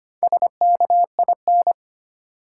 Morse code, heard as a keyed tone, about 25 wpm, 700 Hz. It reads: SKIN
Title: Sentences from Top 1300 Words - New Word Drill - 25wpm